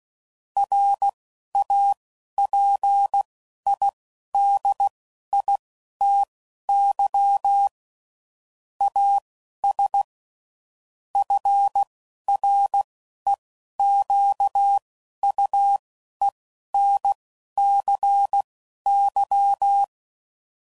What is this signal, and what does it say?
Morse code, heard as a keyed tone, about 16 wpm, spaced as for 11 wpm, 800 Hz. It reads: RAPIDITY AS FREQUENCY
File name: CW Code Groups @10WPM / Technical Subjects / XL